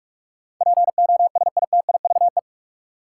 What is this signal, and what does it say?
Morse code, heard as a keyed tone, about 42 wpm, 700 Hz. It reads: POSITIVE
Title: Sentences from Top 1400 Words - New Word Drill Alt - 45wpm